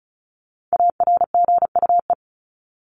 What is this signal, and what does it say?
Morse code, heard as a keyed tone, about 34 wpm, 700 Hz. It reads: ARGUE
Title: Top 1501-1750 Words 35wpm